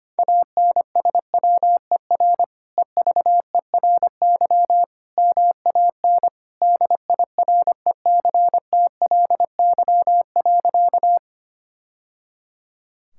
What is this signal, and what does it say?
Morse code, heard as a keyed tone, about 25 wpm, 700 Hz. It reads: ANSWER EVERY MAN DIRECTLY.
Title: Sentences from Top 1000 Words - Review Part 5 - 25wpm